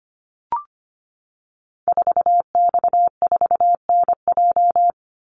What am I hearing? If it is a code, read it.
Morse code, heard as a keyed tone, about 25 wpm, 700 Hz. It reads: 4X4NJ
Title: Call Signs - 25wpm